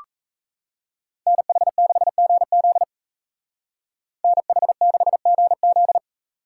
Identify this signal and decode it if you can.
Morse code, heard as a keyed tone, about 36 wpm, 700 Hz. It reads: NH6GZ NH6GZ